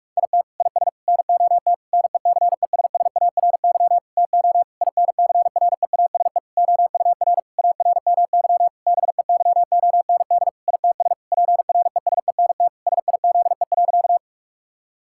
Morse code, heard as a keyed tone, about 42 wpm, 700 Hz. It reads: IT IS NOT NECESSARY TO INCREASE OUR ARMY BEYOND ITS PRESENT SIZE.
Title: Sentences from Top 1000 Words - Review Part 6 - 45wpm